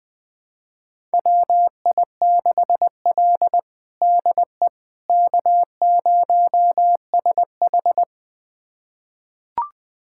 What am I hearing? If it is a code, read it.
Morse code, heard as a keyed tone, about 20 wpm, 700 Hz. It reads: WI6L DE K0SH
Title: USA Callsign DE Callsign - 4 Characters - 20wpm